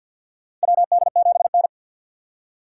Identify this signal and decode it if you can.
Morse code, heard as a keyed tone, about 47 wpm, 700 Hz. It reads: WD7N
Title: USA Call Signs - 4 Characters - 2x Character Spacing - 50wpm